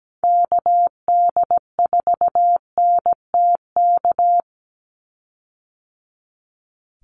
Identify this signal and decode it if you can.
Morse code, heard as a keyed tone, about 17 wpm, 700 Hz. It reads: KD4NTK
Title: USA Call Signs 17wpm